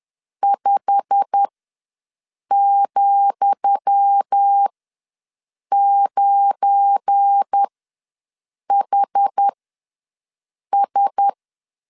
Morse code, heard as a keyed tone, about 10 wpm, spaced as for 6 wpm, 800 Hz. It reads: 5,9HS
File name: LEZIONI / 10 SIGNS / 10D COMA ws 10wpm 4314